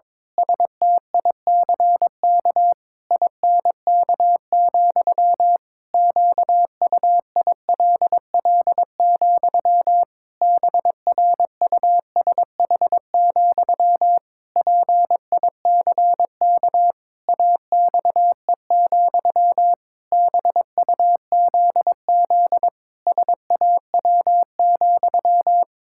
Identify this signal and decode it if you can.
Morse code, heard as a keyed tone, about 22 wpm, 700 Hz. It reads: STICK INK, QUILL, BRUSH, PICK AXE, BUZZ SAW,
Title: Pangram Challenge v2 - 22wpm